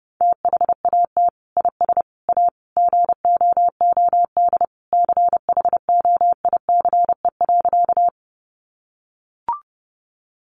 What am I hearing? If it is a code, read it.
Morse code, heard as a keyed tone, about 30 wpm, 700 Hz. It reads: THAT IS A GOOD CHOICE.
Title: Sentences from Top 900 Words 30wpm